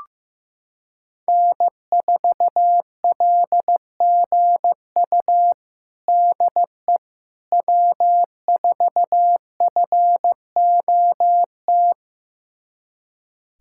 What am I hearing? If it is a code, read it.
Morse code, heard as a keyed tone, about 15 wpm, 700 Hz. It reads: N4LGU DE W4FOT
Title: USA Callsign DE Callsign - 5 Characters - 15wpm